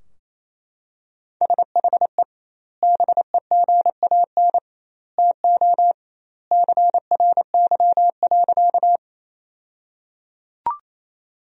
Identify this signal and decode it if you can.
Morse code, heard as a keyed tone, about 28 wpm, 700 Hz. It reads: SHE BEGAN TO CRY.
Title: 2x Word-Spacing - Sentences from Top 500 Words - 28wpm